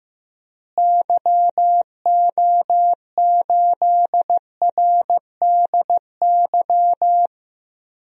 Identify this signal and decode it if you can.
Morse code, heard as a keyed tone, about 15 wpm, 700 Hz. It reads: YO8RDY
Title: DXCC-100 Call Signs - 6 Characters - Repeated 3x - 15wpm